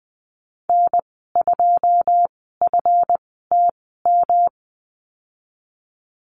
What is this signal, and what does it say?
Morse code, heard as a keyed tone, about 19 wpm, spaced as for 14 wpm, 700 Hz. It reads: N2FTM
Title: 173 Lesson 28g - Number 9 - Call Signs 20wpm